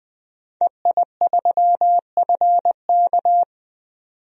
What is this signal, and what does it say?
Morse code, heard as a keyed tone, about 20 wpm, 700 Hz. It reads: EI3FK
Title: DXCC-100 Call Signs - 5 Characters - 20wpm